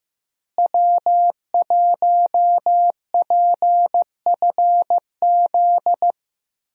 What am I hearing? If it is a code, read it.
Morse code, heard as a keyed tone, about 15 wpm, 700 Hz. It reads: W1PFZ
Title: USA Call Signs - 5 Characters - Repeated 3x - 15wpm